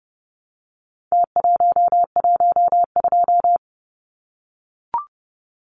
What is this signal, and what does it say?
Morse code, heard as a keyed tone, about 30 wpm, 700 Hz. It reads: T112
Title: Numbers - Contest Style - 4-Digit 30wpm